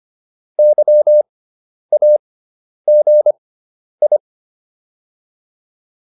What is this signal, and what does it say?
Morse code, heard as a keyed tone, about 25 wpm, spaced as for 10 wpm, 600 Hz. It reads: YAGI